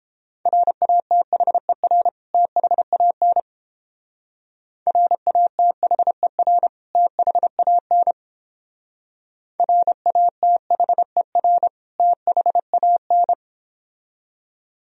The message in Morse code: RATHER THAN RATHER THAN RATHER THAN

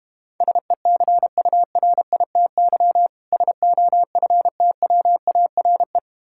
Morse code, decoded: SECURITY SOFTWARE

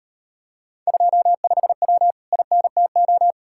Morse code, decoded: 25W INTO

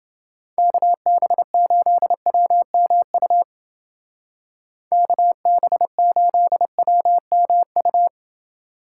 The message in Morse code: KB8WMU KB8WMU

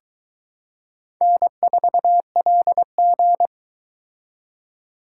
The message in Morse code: N4LG

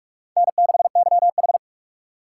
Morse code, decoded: N6YH

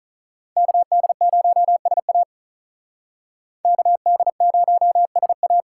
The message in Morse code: KD0SA KD0SA

